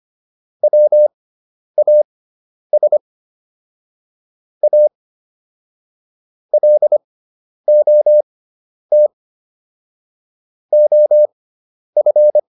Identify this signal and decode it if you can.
Morse code, heard as a keyed tone, about 25 wpm, spaced as for 10 wpm, 600 Hz. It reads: WAS A LOT OF